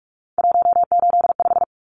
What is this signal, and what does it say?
Morse code, heard as a keyed tone, about 44 wpm, 700 Hz. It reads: 185
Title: Numbers - 3-Digit 45wpm